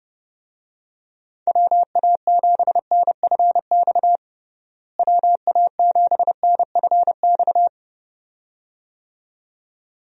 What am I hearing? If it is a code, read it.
Morse code, heard as a keyed tone, about 29 wpm, 700 Hz. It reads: WA7NFX WA7NFX
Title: USA Call Signs - 6 Characters - Repeated 2x - 30wpm